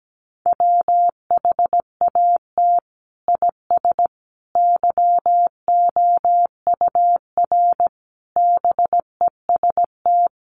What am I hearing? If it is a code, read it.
Morse code, heard as a keyed tone, about 17 wpm, 700 Hz. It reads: WHAT IS YOUR BEST
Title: Sentences from Top 1200 Words 17wpm